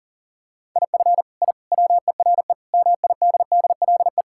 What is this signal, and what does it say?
Morse code, heard as a keyed tone, about 40 wpm, 700 Hz. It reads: IF I WERE MIDDLE
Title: Sentences from Top 1000 Words - Review Part 6 - 40wpm